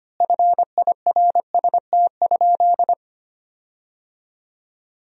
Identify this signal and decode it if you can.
Morse code, heard as a keyed tone, about 25 wpm, 700 Hz. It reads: FIRST?
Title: 4x Word-Spacing - Sentences from Top 100 Words - 25wpm